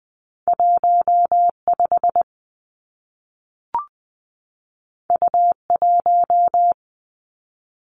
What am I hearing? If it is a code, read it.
Morse code, heard as a keyed tone, about 20 wpm, 700 Hz. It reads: 1 5 U 1
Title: Two Challenging Letter-Number Combinations 20wpm